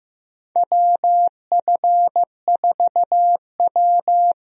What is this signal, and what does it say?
Morse code, heard as a keyed tone, about 15 wpm, 700 Hz. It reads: WF4W
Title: USA Call Signs - 4 Characters - Repeated 2x - 15wpm